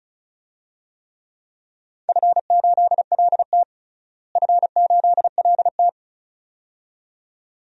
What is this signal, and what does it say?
Morse code, heard as a keyed tone, about 36 wpm, 700 Hz. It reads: F8LT F8LT
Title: DXCC-100 Call Signs - 4 Characters - Repeated 2x - 35wpm